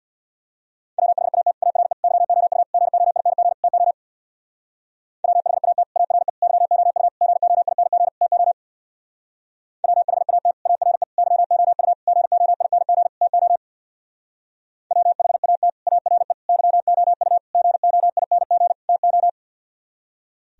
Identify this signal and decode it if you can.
Morse code, heard as a keyed tone, about 58 wpm, 700 Hz. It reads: WHAT ARE YOU GOING TO WHAT ARE YOU GOING TO WHAT ARE YOU GOING TO WHAT ARE YOU GOING TO
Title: Speed-Racing - Sets of 5 Words 50wpm